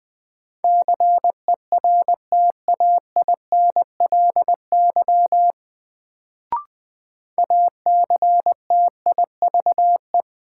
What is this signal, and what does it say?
Morse code, heard as a keyed tone, about 21 wpm, 700 Hz. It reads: CERTAINLY ACTIVE
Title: Sentences from Top 900 Words - New Word Drill - 20wpm